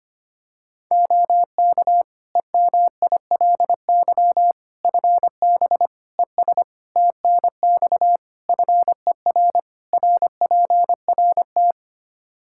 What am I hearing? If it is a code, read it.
Morse code, heard as a keyed tone, about 25 wpm, 700 Hz. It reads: OK EMILY FB ES TNX FER RPRT
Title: Practice QSO - Exchange 2 - Info - 25wpm